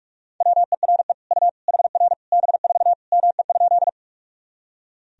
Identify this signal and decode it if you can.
Morse code, heard as a keyed tone, about 46 wpm, 700 Hz. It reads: WERE U HR B4 ME?
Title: Sentences from Top 200 Words and Abbreviations 45wpm